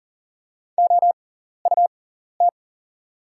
Morse code, heard as a keyed tone, about 40 wpm, spaced as for 14 wpm, 700 Hz. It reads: OUT
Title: 6x Character Spacing - Top 200 Words - 40wpm